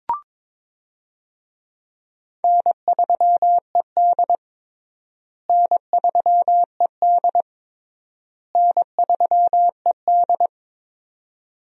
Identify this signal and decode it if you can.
Morse code, heard as a keyed tone, about 22 wpm, 700 Hz. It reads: N3ED N3ED N3ED
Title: USA Call Signs - 4 Characters - Repeated 3x - 22wpm